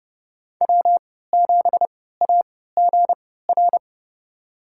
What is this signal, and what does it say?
Morse code, heard as a keyed tone, about 30 wpm, spaced as for 17 wpm, 700 Hz. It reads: W7AGR
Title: USA Call Signs - 5 Characters - 3x Character Spacing - 30wpm